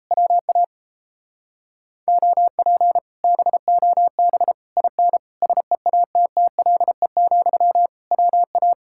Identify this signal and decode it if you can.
Morse code, heard as a keyed tone, about 38 wpm, 700 Hz. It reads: WA OP BOB IN SEATTLE, WA